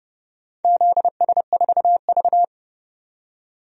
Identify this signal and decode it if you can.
Morse code, heard as a keyed tone, about 30 wpm, 700 Hz. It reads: ZS4V
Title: DXCC-100 Call Signs - 4 Characters - 30wpm